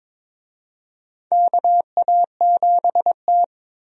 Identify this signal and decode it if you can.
Morse code, heard as a keyed tone, about 22 wpm, 700 Hz. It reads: KA7T
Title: USA Call Signs - 4 Characters - Repeated 2x - 22wpm